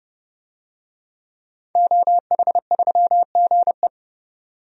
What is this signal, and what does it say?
Morse code, heard as a keyed tone, about 30 wpm, 700 Hz. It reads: OH3GE
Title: DXCC-100 Call Signs - 5 Characters - 30wpm